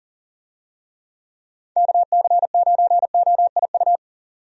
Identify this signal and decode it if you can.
Morse code, heard as a keyed tone, about 40 wpm, 700 Hz. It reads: KC9OIU
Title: USA Call Signs - 6 Characters - 40wpm